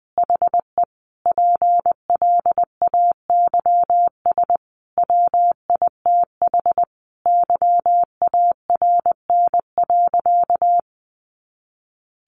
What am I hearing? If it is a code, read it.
Morse code, heard as a keyed tone, about 20 wpm, 700 Hz. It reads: HE PLAYS WITH YARN.
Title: Easy Phrases 20wpm